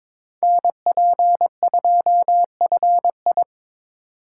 Morse code, heard as a keyed tone, about 25 wpm, 700 Hz. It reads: NP2FI
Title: Speed Racing - USA 2x2 Call Signs 22wpm